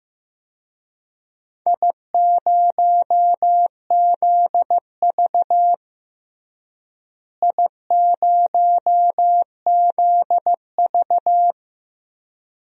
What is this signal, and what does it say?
Morse code, heard as a keyed tone, about 15 wpm, 700 Hz. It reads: I0ZV I0ZV
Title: DXCC-100 Call Signs - 4 Characters - Repeated 2x - 15wpm